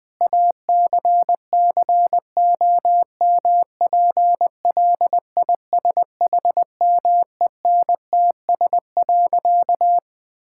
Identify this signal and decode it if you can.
Morse code, heard as a keyed tone, about 21 wpm, 700 Hz. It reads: ACCOMPLISHMENTS.
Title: Difficult Sentences with Hints - 20wpm